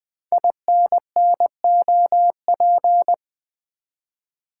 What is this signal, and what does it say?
Morse code, heard as a keyed tone, about 20 wpm, 700 Hz. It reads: INNOP
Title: Code Groups - 5 Letters - 20wpm